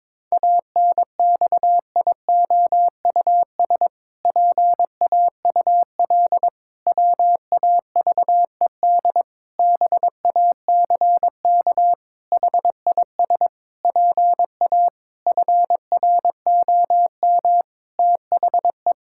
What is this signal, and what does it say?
Morse code, heard as a keyed tone, about 22 wpm, 700 Hz. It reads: ANXIOUS PAUL WAVED BACK HIS PA FROM THE
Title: Pangram Challenge v2 - 22wpm